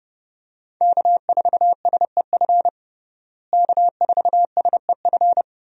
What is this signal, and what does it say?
Morse code, heard as a keyed tone, about 31 wpm, 700 Hz. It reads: K4SEF K4SEF
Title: USA Call Signs - 5 Characters - Repeated 2x - 30wpm